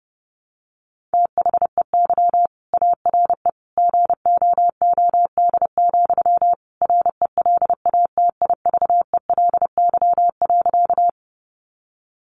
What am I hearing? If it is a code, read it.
Morse code, heard as a keyed tone, about 30 wpm, 700 Hz. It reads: THEY ARE GOOD, RELATIVELY.
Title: Sentences from Top 1100 Words 30wpm